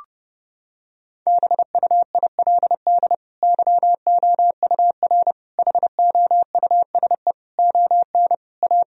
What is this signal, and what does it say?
Morse code, heard as a keyed tone, about 30 wpm, 700 Hz. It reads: BUILD YOUR HOUSE ON A
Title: Sentences from Top 1000 Words - Review Part 2 - 30wpm